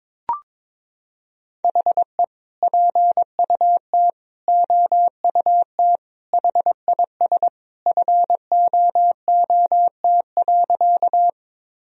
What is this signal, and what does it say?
Morse code, heard as a keyed tone, about 22 wpm, 700 Hz. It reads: HE PUT OUT HIS FOOT.
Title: Sentences from Top 1000 Words - Review Part 3 - 22wpm